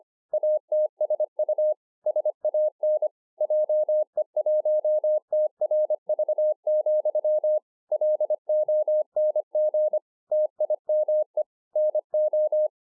Morse code, heard as a keyed tone, about 25 wpm, 610 Hz. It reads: ATSU SAN JE1TRV, LONG TIME NO